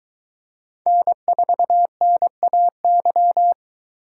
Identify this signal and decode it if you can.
Morse code, heard as a keyed tone, about 22 wpm, 700 Hz. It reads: N4NAY